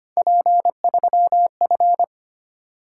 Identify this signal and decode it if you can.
Morse code, heard as a keyed tone, about 25 wpm, 700 Hz. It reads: P3F
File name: Three Letter-Number Combinations - 25wpm